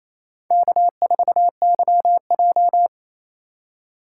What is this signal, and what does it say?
Morse code, heard as a keyed tone, about 28 wpm, 700 Hz. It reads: K4YJ
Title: USA Call Signs - 4 Characters - 4x Character Spacing - 28wpm